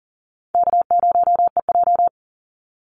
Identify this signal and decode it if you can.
Morse code, heard as a keyed tone, about 40 wpm, 700 Hz. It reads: K0EJ
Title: Easy Call Signs Repeated 3x - 40wpm